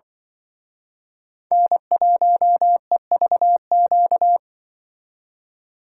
Morse code, heard as a keyed tone, about 25 wpm, 700 Hz. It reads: N1EVQ